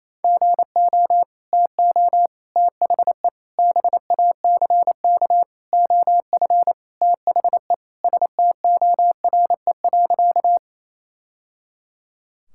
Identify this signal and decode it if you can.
Morse code, heard as a keyed tone, about 27 wpm, 700 Hz. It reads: GO TO THE BACK OF THE STORE.
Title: Sentences from Top 1000 Words - Review Part 5 - 28wpm